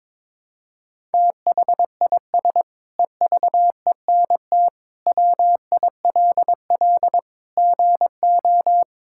Morse code, heard as a keyed tone, about 22 wpm, 700 Hz. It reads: THIS EVENT WILL GO
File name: Sentences with 9-Word Phrases - 22wpm